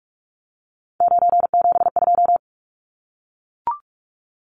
Morse code, heard as a keyed tone, about 44 wpm, 700 Hz. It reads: 972
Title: Numbers - 3-Digit 45wpm